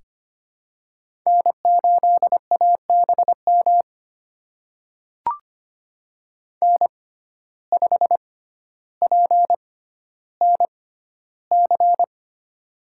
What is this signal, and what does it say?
Morse code, heard as a keyed tone, about 25 wpm, spaced as for 13 wpm, 700 Hz. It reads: N8ABM N5PNC
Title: USA Call Signs - 5 Characters - 6x Character Spacing - 25wpm